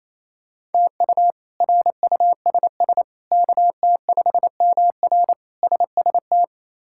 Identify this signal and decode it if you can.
Morse code, heard as a keyed tone, about 28 wpm, 700 Hz. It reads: TU RUSS KT5MR SST
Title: K1USN SST QSO Practice - 28wpm